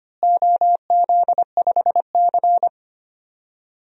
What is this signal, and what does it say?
Morse code, heard as a keyed tone, about 25 wpm, 700 Hz. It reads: OZ5C
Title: DXCC-100 Call Signs - 4 Characters - Repeated 2x - 25wpm